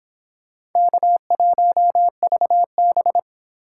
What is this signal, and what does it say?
Morse code, heard as a keyed tone, about 22 wpm, 700 Hz. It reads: K1VB